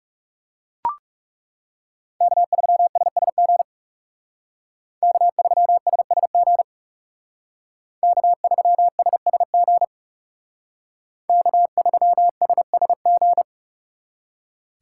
Morse code, heard as a keyed tone, about 39 wpm, 700 Hz. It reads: K3SSG K3SSG K3SSG K3SSG